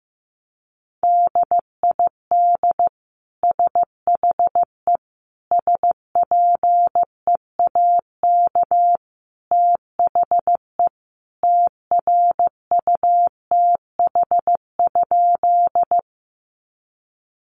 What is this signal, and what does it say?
Morse code, heard as a keyed tone, about 15 wpm, 700 Hz. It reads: DID SHE SPEAK THE TRUTH?
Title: Sentences from Top 1100 Words 15wpm